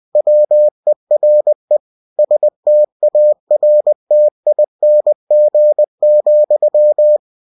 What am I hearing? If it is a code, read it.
Morse code, heard as a keyed tone, about 20 wpm, 600 Hz. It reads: WERE STARTING,
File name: cw-books / winnie-the-pooh / winnie-20wpm-page-0080